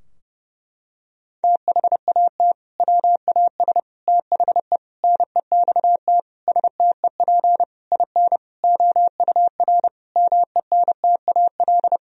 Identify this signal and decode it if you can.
Morse code, heard as a keyed tone, about 30 wpm, 700 Hz. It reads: THAT WAS THE NEXT STEP IN OUR MENTAL
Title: Sentences from Top 1000 Words - Review Part 4 - 30wpm